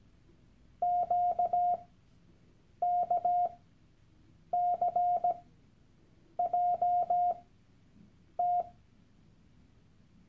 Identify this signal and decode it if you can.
Morse code, heard as a keyed tone, about 17 wpm, spaced as for 7 wpm, 710 Hz. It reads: QKCJT
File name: Workshop Dateien / CW-Online-Kurs / Lektion 001